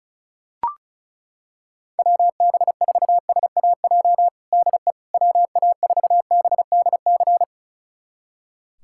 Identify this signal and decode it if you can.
Morse code, heard as a keyed tone, about 34 wpm, 700 Hz. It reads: WB4SAJ DE WA4BDC
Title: USA Callsign DE Callsign - 6 Characters - 35wpm